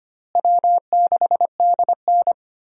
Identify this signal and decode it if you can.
Morse code, heard as a keyed tone, about 25 wpm, 700 Hz. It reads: W6DN
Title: USA Call Signs - 4 Characters - Repeated 2x - 25wpm